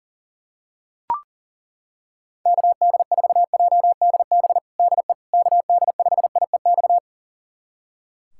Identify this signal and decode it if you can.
Morse code, heard as a keyed tone, about 40 wpm, 700 Hz. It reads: KD4JDB DE KD5IEX
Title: USA Callsign DE Callsign 40wpm